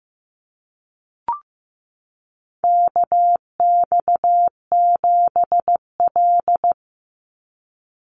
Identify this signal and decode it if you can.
Morse code, heard as a keyed tone, about 15 wpm, 700 Hz. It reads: KX7L
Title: NA SOTA Chaser Callsigns 15wpm